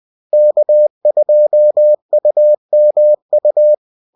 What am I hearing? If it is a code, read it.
Morse code, heard as a keyed tone, about 20 wpm, 600 Hz. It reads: K2UMU